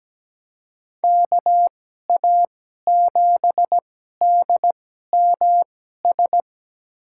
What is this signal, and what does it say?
Morse code, heard as a keyed tone, about 17 wpm, spaced as for 12 wpm, 700 Hz. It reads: KA7DMS
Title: USA Call Signs - 6 Characters - 2x Character Spacing - 17wpm